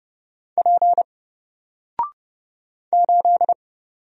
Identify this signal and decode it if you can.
Morse code, heard as a keyed tone, about 30 wpm, 700 Hz. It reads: P 8
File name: Single-Letter-Number-30wpm